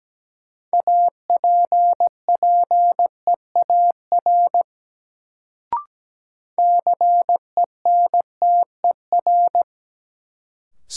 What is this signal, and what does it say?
Morse code, heard as a keyed tone, about 17 wpm, 700 Hz. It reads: APPEAR CENTER
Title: Sentences from Top 1000 Words - Review Part 2 - New Word Drill - 17wpm